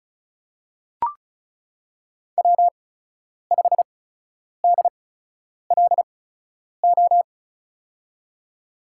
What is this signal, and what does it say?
Morse code, heard as a keyed tone, about 36 wpm, spaced as for 10 wpm, 700 Hz. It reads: W5DLO
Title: USA Call Signs - 5 Characters - 8x Character Spacing - 35wpm